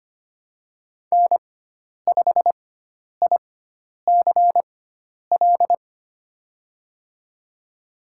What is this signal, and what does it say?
Morse code, heard as a keyed tone, about 25 wpm, spaced as for 10 wpm, 700 Hz. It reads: N5ICL
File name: Lesson 16g - Letter c - Call Signs 25wpm